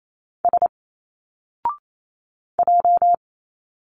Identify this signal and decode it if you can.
Morse code, heard as a keyed tone, about 28 wpm, 700 Hz. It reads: S J
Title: Single Letters 28wpm